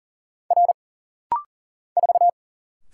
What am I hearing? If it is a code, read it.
Morse code, heard as a keyed tone, about 40 wpm, 700 Hz. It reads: R 4
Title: Single Letter-Number - Rapid-Fire 40wpm